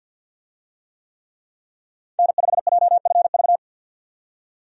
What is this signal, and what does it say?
Morse code, heard as a keyed tone, about 50 wpm, 700 Hz. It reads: N5JFV
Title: USA Call Signs - 5 Characters - 50wpm